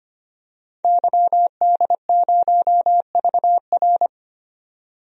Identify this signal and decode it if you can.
Morse code, heard as a keyed tone, about 25 wpm, 700 Hz. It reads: YD0VR
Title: DXCC-100 Call Signs - 5 Characters - 25wpm